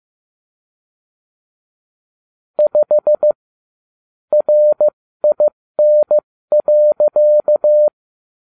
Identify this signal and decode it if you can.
Morse code, heard as a keyed tone, about 15 wpm, spaced as for 13 wpm, 600 Hz. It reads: 5 RIN.